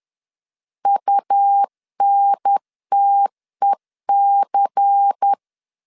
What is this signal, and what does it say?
Morse code, heard as a keyed tone, about 10 wpm, 800 Hz. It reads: UNTEC
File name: LEZIONI / 7 GF9Z / 7 GF9 10wpm 2747